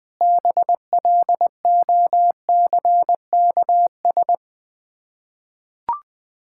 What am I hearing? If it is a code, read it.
Morse code, heard as a keyed tone, about 21 wpm, 700 Hz. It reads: BLOCKS
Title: Difficult Words with Hints - 20wpm